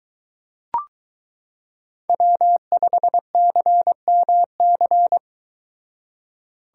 Morse code, heard as a keyed tone, about 23 wpm, 700 Hz. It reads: W5CMC